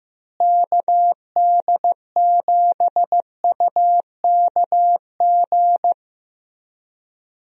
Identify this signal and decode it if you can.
Morse code, heard as a keyed tone, about 15 wpm, 700 Hz. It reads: KD7UKG
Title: USA Call Signs - 6 Characters - Repeated 2x - 15wpm